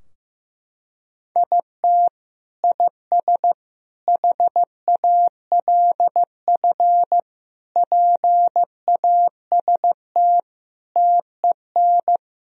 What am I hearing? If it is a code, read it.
Morse code, heard as a keyed tone, about 15 wpm, 700 Hz. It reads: IT IS HALF PAST TEN
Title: Sentences from Top 1000 Words - Review Part 3 - 15wpm